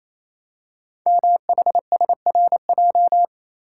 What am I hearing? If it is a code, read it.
Morse code, heard as a keyed tone, about 27 wpm, 700 Hz. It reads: MHSRJ
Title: Code Groups - 5 Letters - 28wpm